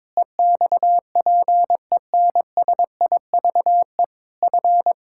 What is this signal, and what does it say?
Morse code, heard as a keyed tone, about 22 wpm, 700 Hz. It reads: EXPENSIVE F
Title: Sentences from Top 1000 Words - Review Part 6 - 22wpm